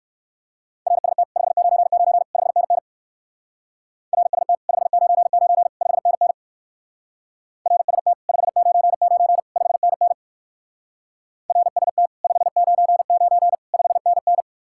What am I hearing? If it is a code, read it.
Morse code, heard as a keyed tone, about 58 wpm, 700 Hz. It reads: RST 599 5NN RST 599 5NN RST 599 5NN RST 599 5NN